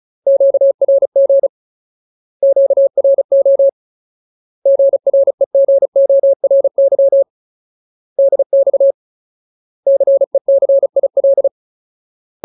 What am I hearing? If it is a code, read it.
Morse code, heard as a keyed tone, about 34 wpm, 550 Hz. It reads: QRG QRO GREGORY DX CECIL
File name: Ultimate Word List / 35WPM Ultimate